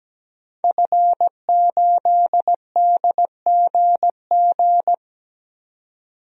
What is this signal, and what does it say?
Morse code, heard as a keyed tone, about 17 wpm, 700 Hz. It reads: F8DGG
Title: DXCC-100 Call Signs - 5 Characters - 17wpm